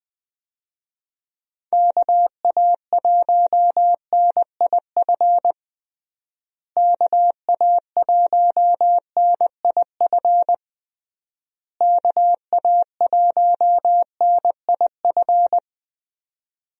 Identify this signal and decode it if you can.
Morse code, heard as a keyed tone, about 20 wpm, 700 Hz. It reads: KA1NIF KA1NIF KA1NIF
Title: USA Call Signs - 6 Characters - Repeated 3x - 20wpm